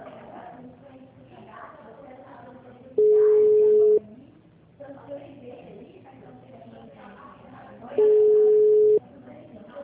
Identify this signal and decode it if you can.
Ringback tone (Europe, 425 Hz), with speech.
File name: out- -2013- - - .34937